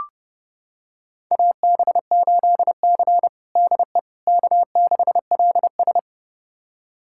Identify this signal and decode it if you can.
Morse code, heard as a keyed tone, about 32 wpm, 700 Hz. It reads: AB8C DE K6LS
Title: USA Callsign DE Callsign - 4 Characters - 30wpm